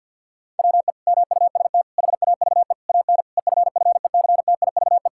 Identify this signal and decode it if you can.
Morse code, heard as a keyed tone, about 50 wpm, 700 Hz. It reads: WE MUST HAVE AN EFFECTIVE
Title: Sentences from Top 1000 Words - Review Part 6 - 50wpm